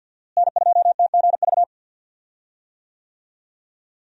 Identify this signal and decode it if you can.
Morse code, heard as a keyed tone, about 52 wpm, 700 Hz. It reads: N2TGV
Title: USA Call Signs - 5 Characters - Repeated 3x - 50wpm